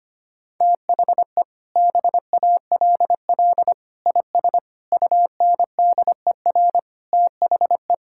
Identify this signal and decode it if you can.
Morse code, heard as a keyed tone, about 25 wpm, 700 Hz. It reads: THE BALL IS UNDER THE
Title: Sentences from Top 1000 Words - Review Part 6 - 25wpm